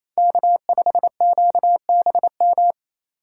Morse code, heard as a keyed tone, about 28 wpm, 700 Hz. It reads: K5QBM
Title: USA Call Signs - 5 Characters - 4x Character Spacing - 28wpm